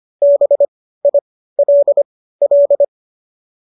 Morse code, heard as a keyed tone, about 25 wpm, spaced as for 15 wpm, 590 Hz. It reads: BILL